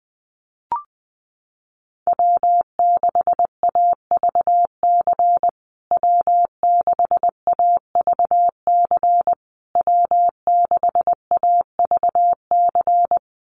Morse code, heard as a keyed tone, about 20 wpm, 700 Hz. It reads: W6AVC W6AVC W6AVC
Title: USA Call Signs Repeated 3x 20wpm